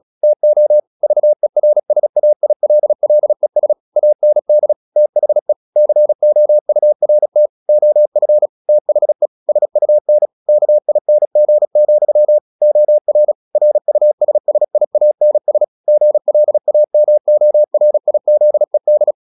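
Morse code, heard as a keyed tone, about 36 wpm, 600 Hz. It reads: TO VERSAILLES AND THE COURT OF THE SUN KING, OR RUSSIANS GLAMORIZED
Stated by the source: Morse practice at 36 wpm, text of Ken Follett's novel Never, from CW med Gnister.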